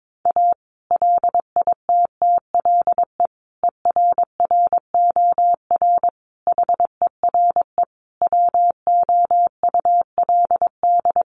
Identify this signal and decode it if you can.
Morse code, heard as a keyed tone, about 22 wpm, 700 Hz. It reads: A LITTLE ERROR HERE WOULD
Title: A Sound Of Thunder 22wpm